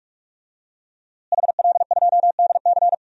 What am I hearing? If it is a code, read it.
Morse code, heard as a keyed tone, about 43 wpm, 700 Hz. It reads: HL2DC
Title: DXCC-100 Call Signs - 5 Characters - Repeated 2x - 45wpm